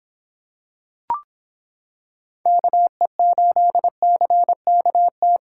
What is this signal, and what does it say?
Morse code, heard as a keyed tone, about 26 wpm, 700 Hz. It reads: KE8CKT